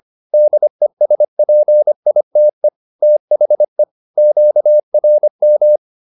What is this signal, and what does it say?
Morse code, heard as a keyed tone, about 25 wpm, 600 Hz. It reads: DESPITE THE QRM